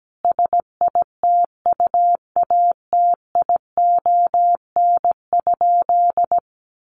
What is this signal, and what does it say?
Morse code, heard as a keyed tone, about 17 wpm, 700 Hz. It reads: SITUATION?
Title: Sentences from Top 1200 Words 17wpm